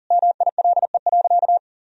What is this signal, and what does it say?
Morse code, heard as a keyed tone, about 40 wpm, 700 Hz. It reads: MILE.
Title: Sentences from Top 1000 Words - Review Part 4 - 40wpm